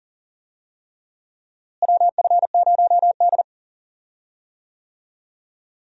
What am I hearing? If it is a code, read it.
Morse code, heard as a keyed tone, about 40 wpm, 700 Hz. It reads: WF0D
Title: USA Call Signs - 4 Characters - Repeated 3x - 40wpm